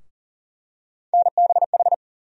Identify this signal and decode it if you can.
Morse code, heard as a keyed tone, about 40 wpm, 700 Hz. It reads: NBH
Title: USA Call Sign Postfixes - 3 Letters 40wpm